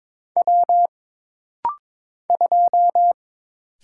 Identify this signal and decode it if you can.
Morse code, heard as a keyed tone, about 22 wpm, 700 Hz. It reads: W 2
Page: This rapid-fire timing is geared to push your ICR (Instant Character Recognition) skills.